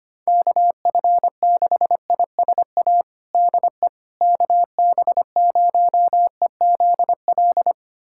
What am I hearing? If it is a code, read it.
Morse code, heard as a keyed tone, about 25 wpm, 700 Hz. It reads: KF6ISA DE KB0EZL
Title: USA Callsign DE Callsign - 6 Characters - 25wpm